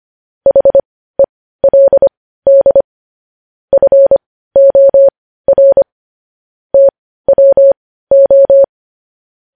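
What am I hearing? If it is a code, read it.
Morse code, heard as a keyed tone, about 25 wpm, spaced as for 15 wpm, 550 Hz. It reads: HELD FOR TWO
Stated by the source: Morse code news headlines sent at 15.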